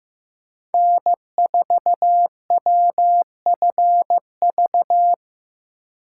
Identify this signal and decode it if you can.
Morse code, heard as a keyed tone, about 15 wpm, 700 Hz. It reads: N4WFV
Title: USA Call Signs - 4x Character Spacing - 15wpm